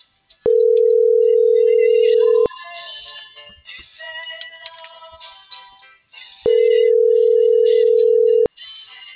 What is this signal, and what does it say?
Ringback tone (North America), 440 + 480 Hz.